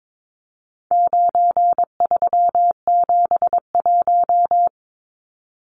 Morse code, heard as a keyed tone, about 22 wpm, 700 Hz. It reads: 9371
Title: Numbers - 4-Digit 22wpm